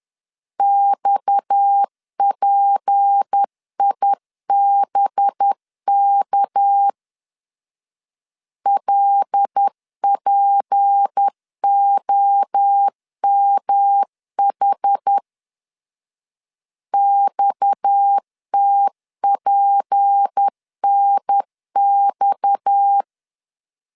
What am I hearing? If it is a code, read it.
Morse code, heard as a keyed tone, about 10 wpm, 800 Hz. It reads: XPIBK LPOMH XTPNX
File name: LEZIONI / 6 PKX0 / 6 PKX 10wpm 2426